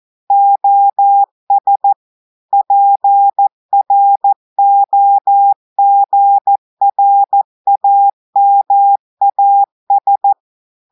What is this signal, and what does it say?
Morse code, heard as a keyed tone, about 14 wpm, 800 Hz. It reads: OS PROGRAMAS
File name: audios / portugais / portugais texte en clair 2 vitesse 14 mots minutes